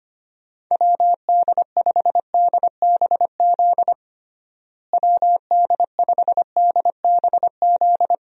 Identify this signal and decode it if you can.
Morse code, heard as a keyed tone, about 25 wpm, 700 Hz. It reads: WD5DBZ WD5DBZ
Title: USA Call Signs - 6 Characters - Repeated 2x - 25wpm